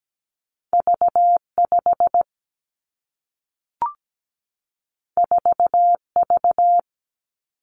Morse code, heard as a keyed tone, about 17 wpm, 700 Hz. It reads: V 5 4 V
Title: Two Challenging Letter-Number Combinations 17wpm